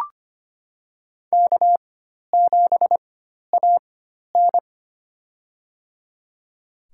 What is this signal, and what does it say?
Morse code, heard as a keyed tone, about 25 wpm, spaced as for 12 wpm, 700 Hz. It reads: K7AN
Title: USA Call Signs - 4 Characters - 4x Character Spacing - 25wpm